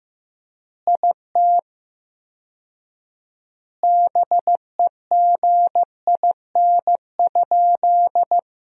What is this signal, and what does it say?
Morse code, heard as a keyed tone, about 15 wpm, 700 Hz. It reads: IT BEGIN?
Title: 4x Word-Spacing - Sentences from Top 300 Words - 15wpm